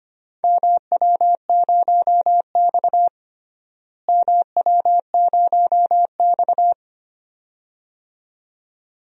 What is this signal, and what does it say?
Morse code, heard as a keyed tone, about 25 wpm, 700 Hz. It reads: MW0X MW0X
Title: DXCC-100 Call Signs - 4 Characters - Repeated 2x - 25wpm